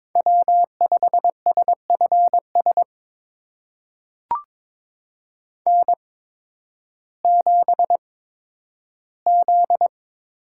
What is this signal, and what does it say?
Morse code, heard as a keyed tone, about 22 wpm, spaced as for 12 wpm, 700 Hz. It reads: W5SFS N7Z
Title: USA Call Signs - 5 Characters - 8x Character Spacing - 22wpm